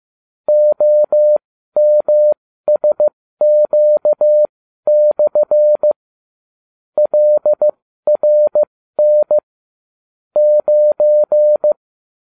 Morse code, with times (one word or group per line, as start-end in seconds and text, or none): 0.5-5.9 OMSQ/
7.0-9.4 LRN
10.4-11.7 9